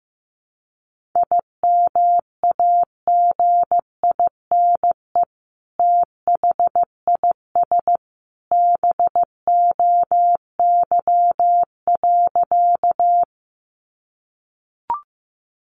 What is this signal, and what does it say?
Morse code, heard as a keyed tone, about 14 wpm, 700 Hz. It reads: IMAGINE THIS BOY.
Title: Sentences from Top 1400 Words 15wpm